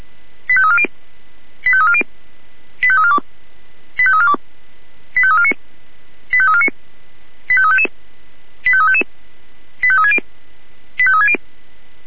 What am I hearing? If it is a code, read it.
Five-tone selective call (ZVEI), heard as quick runs of five tones, tones of about 70 ms. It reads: 85340 85340 85342 85342 85349 85349 85355 85355 85360 85360